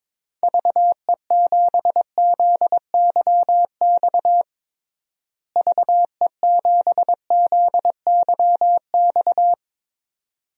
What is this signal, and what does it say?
Morse code, heard as a keyed tone, about 22 wpm, 700 Hz. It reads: VE7ZYX VE7ZYX
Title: DXCC-100 Call Signs - 6 Characters - Repeated 2x - 22wpm